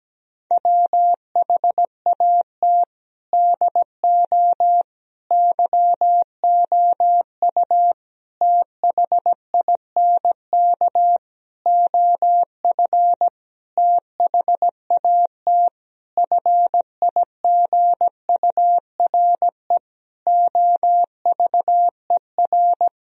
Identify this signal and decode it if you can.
Morse code, heard as a keyed tone, about 17 wpm, 700 Hz. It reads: WHAT DO YOU THINK OF THAT FIGURE OVER
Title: Sentences from Top 1000 Words - Review Part 2 - 17wpm